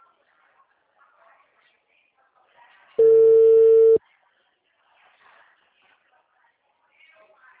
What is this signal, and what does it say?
Ringback tone (China, 450 Hz), with speech.